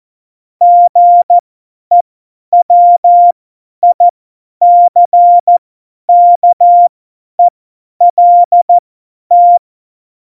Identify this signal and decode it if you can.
Morse code, heard as a keyed tone, about 14 wpm, spaced as for 10 wpm, 700 Hz. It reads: GEWICKELT
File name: Morsekurs / Witze / Witz 03. Oktober 17, Tempo 50 F70